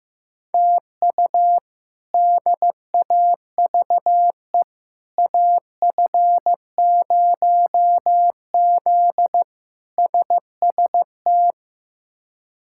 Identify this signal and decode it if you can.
Morse code, heard as a keyed tone, about 15 wpm, 700 Hz. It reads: TU DAVE AF0Z SST
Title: K1USN SST QSO Practice - 15wpm